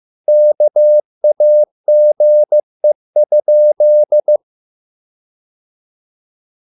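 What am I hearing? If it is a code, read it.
Morse code, heard as a keyed tone, about 15 wpm, 600 Hz. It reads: KAGE?
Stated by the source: Morse code at a slow 15 wpm from CW med Gnister.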